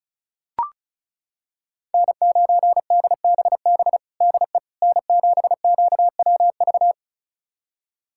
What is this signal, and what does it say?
Morse code, heard as a keyed tone, about 36 wpm, 700 Hz. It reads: N9DBB DE N7QWV
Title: USA Callsign DE Callsign - 5 Characters - 35wpm